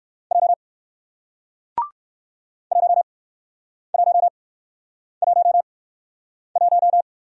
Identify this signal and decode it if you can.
Morse code, heard as a keyed tone, about 57 wpm, 700 Hz. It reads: J 1 1 1 1